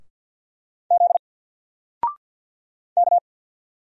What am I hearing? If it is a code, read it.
Morse code, heard as a keyed tone, about 48 wpm, 700 Hz. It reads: Z K
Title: Single Character (Letter:Number:Punctuation:Prosign) Alt 50wpm